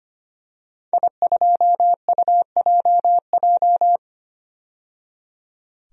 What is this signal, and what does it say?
Morse code, heard as a keyed tone, about 25 wpm, 700 Hz. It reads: I2UJJ
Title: DXCC-100 Call Signs - 5 Characters - 25wpm